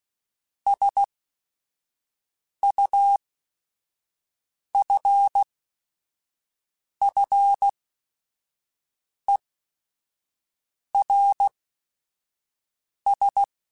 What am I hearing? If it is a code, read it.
Morse code, heard as a keyed tone, about 16 wpm, spaced as for 5 wpm, 800 Hz. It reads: SUFFERS